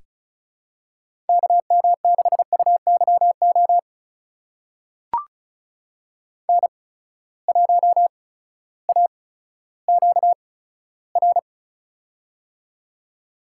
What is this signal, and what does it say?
Morse code, heard as a keyed tone, about 36 wpm, spaced as for 16 wpm, 700 Hz. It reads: KM6UYO N1AQR
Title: USA Call Signs - 8x Character Spacing - 35wpm